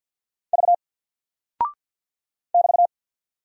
Morse code, heard as a keyed tone, about 50 wpm, 700 Hz. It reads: V <BT>
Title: Single Character (Letter:Number:Punctuation:Prosign) 50wpm